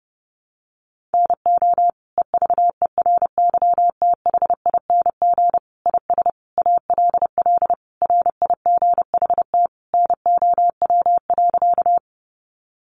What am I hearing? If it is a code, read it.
Morse code, heard as a keyed tone, about 30 wpm, 700 Hz. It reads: NO EVERYTHING IS ALL RIGHT NOW.
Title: Sentences from Top 700 Words 30wpm